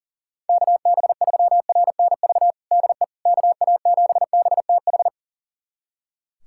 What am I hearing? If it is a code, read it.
Morse code, heard as a keyed tone, about 40 wpm, 700 Hz. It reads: KB3RNV DE KA7BTH
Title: USA Callsign DE Callsign 40wpm